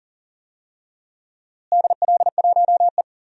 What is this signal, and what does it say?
Morse code, heard as a keyed tone, about 40 wpm, 700 Hz. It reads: DL1E
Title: DXCC-100 Call Signs - 4 Characters - Repeated 2x - 40wpm